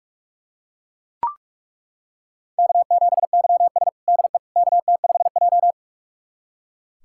Morse code, heard as a keyed tone, about 41 wpm, 700 Hz. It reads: K7YS DE KT5J
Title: USA Callsign DE Callsign - 4 Characters - 45wpm